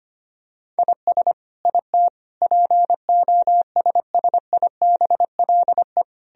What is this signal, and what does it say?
Morse code, heard as a keyed tone, about 25 wpm, 700 Hz. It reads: IS IT POSSIBLE
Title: Sentences from Top 1000 Words - Review Part 3 - 25wpm